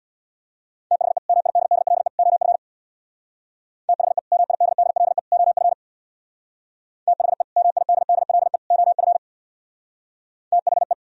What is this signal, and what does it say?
Morse code, heard as a keyed tone, about 64 wpm, 700 Hz. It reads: THE MIDDLE OF THE MIDDLE OF THE MIDDLE OF THE